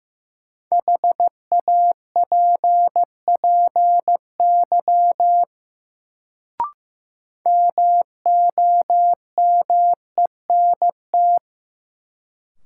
Morse code, heard as a keyed tone, about 15 wpm, 700 Hz. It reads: HAPPY MOMENT
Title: Sentences from Top 1000 Words - New Word Drill - 15wpm